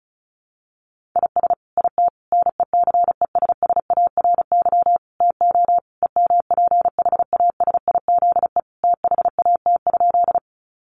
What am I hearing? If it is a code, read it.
Morse code, heard as a keyed tone, about 34 wpm, 700 Hz. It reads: IS IT NECESSARY TO EMPHASIZE THAT?
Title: Sentences from Top 1500 Words 35wpm